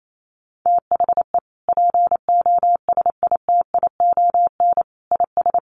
Morse code, heard as a keyed tone, about 28 wpm, 700 Hz. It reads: THE POSITION IS
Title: Sentences from Top 1300 Words 28wpm